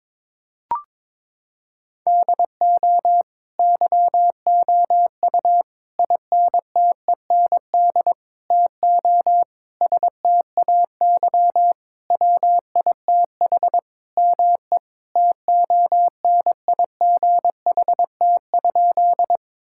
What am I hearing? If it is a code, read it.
Morse code, heard as a keyed tone, about 21 wpm, 700 Hz. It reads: DO YOU INTEND TO STAY WITH ME TONIGHT?
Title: Sentences from Top 1600 Words 22wpm